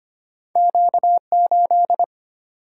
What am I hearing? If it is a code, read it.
Morse code, heard as a keyed tone, about 25 wpm, 690 Hz. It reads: Q8
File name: Two Letter-Number Combinations - 25wpm